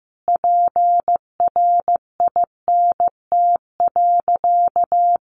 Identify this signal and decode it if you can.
Morse code, heard as a keyed tone, about 15 wpm, 700 Hz. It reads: PRINT.
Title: Sentences from Top 1400 Words 15wpm